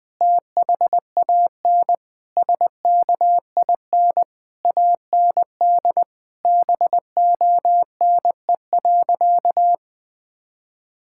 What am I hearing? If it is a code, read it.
Morse code, heard as a keyed tone, about 21 wpm, 700 Hz. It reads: THAN SKIN AND BONE.
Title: Sentences from Top 1600 Words 20wpm